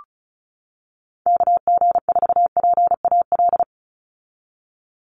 Morse code, heard as a keyed tone, about 34 wpm, 700 Hz. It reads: KG4PAL
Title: USA Call Signs 35wpm